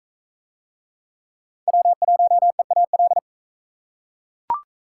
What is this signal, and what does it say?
Morse code, heard as a keyed tone, about 41 wpm, 700 Hz. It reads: W1EAL